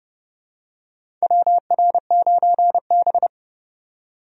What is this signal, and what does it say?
Morse code, heard as a keyed tone, about 31 wpm, 700 Hz. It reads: WR9B
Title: USA Call Signs - 4 Characters - 30wpm